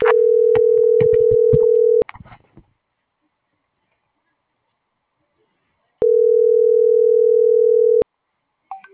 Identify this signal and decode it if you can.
Ringback tone (North America), 440 + 480 Hz.